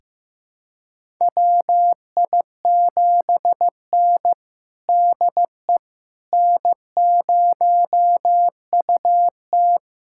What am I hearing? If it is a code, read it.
Morse code, heard as a keyed tone, about 15 wpm, 700 Hz. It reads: WI7N DE N0UT
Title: USA Callsign DE Callsign - 4 Characters - 15wpm